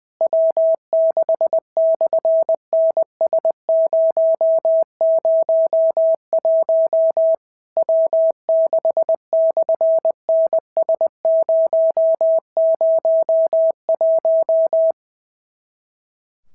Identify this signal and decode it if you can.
Morse code, heard as a keyed tone, about 19 wpm, 650 Hz. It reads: W6/NS001 W6/NS001
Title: SOTA QSO Practice 20wpm